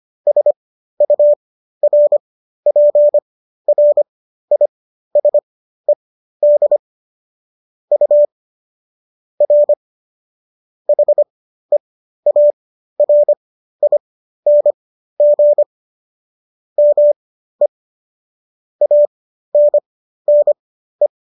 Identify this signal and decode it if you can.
Morse code, heard as a keyed tone, about 25 wpm, spaced as for 13 wpm, 590 Hz. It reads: SURPRISED U R HEARING ME ANNE